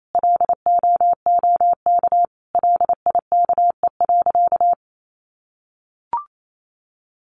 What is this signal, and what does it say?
Morse code, heard as a keyed tone, about 28 wpm, 700 Hz. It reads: LOOK LIKE.
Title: Sentences from Top 1300 Words 28wpm